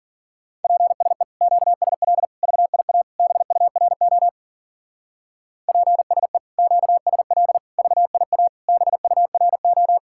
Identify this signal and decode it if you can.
Morse code, heard as a keyed tone, about 52 wpm, 700 Hz. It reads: PSE QSL VIA BURO PSE QSL VIA BURO